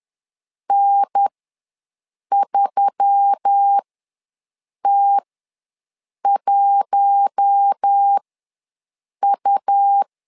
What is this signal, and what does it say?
Morse code, heard as a keyed tone, about 10 wpm, spaced as for 6 wpm, 800 Hz. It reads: N3T1U
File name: LEZIONI / 4 Q1S3 ws 10wpm 2437